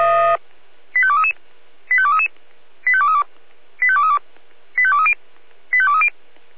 Five-tone selective call (ZVEI), heard as quick runs of five tones, tones of about 70 ms. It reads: 85233 85233 85232 85232 85230 85230
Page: /SDR/ZVEI/